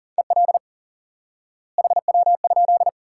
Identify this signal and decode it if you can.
Morse code, heard as a keyed tone, about 40 wpm, 700 Hz. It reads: EL HW?
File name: IQE Course 3 - Lesson 89 - YAGI - Two Elements - 40wpm